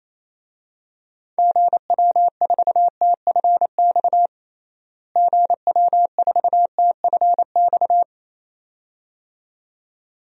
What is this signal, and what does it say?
Morse code, heard as a keyed tone, about 27 wpm, 700 Hz. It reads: GW4TFX GW4TFX
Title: DXCC-100 Call Signs - 6 Characters - Repeated 2x - 28wpm